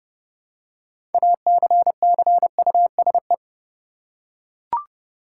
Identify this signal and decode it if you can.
Morse code, heard as a keyed tone, about 32 wpm, 700 Hz. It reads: ACCUSE
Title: Six Letter Words - 30wpm